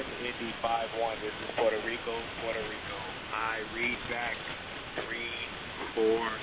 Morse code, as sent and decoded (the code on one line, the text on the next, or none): none